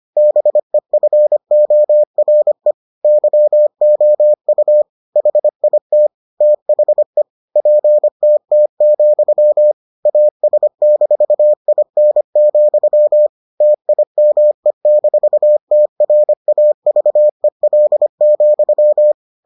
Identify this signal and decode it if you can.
Morse code, heard as a keyed tone, about 25 wpm, 600 Hz. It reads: BEFORE YOU HIT THE PTT, AS-IN, TIME-TRAVEL,